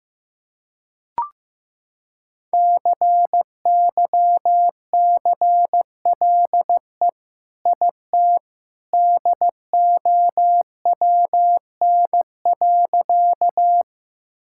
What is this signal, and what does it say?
Morse code, heard as a keyed tone, about 15 wpm, 700 Hz. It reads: CYCLE IT DOWN.
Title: Sentences from Top 1000 Words - Review Part 6 - 15wpm